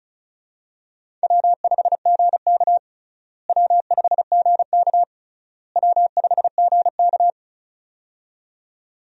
Morse code, heard as a keyed tone, about 34 wpm, 700 Hz. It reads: W5GK W5GK W5GK
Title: USA Call Signs - 4 Characters - Repeated 3x - 35wpm